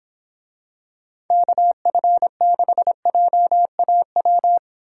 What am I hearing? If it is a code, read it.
Morse code, heard as a keyed tone, about 25 wpm, 700 Hz. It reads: KF6JAW